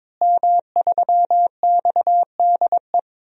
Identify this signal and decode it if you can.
Morse code, heard as a keyed tone, about 22 wpm, 700 Hz. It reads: M3XDE
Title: DXCC-100 Call Signs - 5 Characters - Repeated 2x - 22wpm